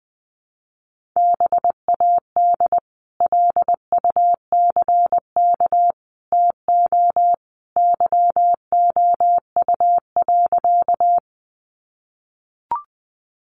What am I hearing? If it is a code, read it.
Morse code, heard as a keyed tone, about 20 wpm, 700 Hz. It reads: BAD LUCK TO YOU.
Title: Sentences from Top 1500 Words 20wpm